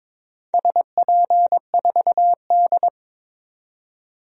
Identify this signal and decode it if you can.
Morse code, heard as a keyed tone, about 23 wpm, 690 Hz. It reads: SP4D
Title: DXCC-100 Call Signs - 4 Characters - 22wpm